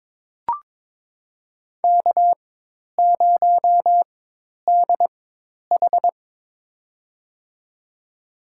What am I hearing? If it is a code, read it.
Morse code, heard as a keyed tone, about 22 wpm, spaced as for 10 wpm, 700 Hz. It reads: K0DH
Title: USA Call Signs - 4 Characters - 4x Character Spacing - 22wpm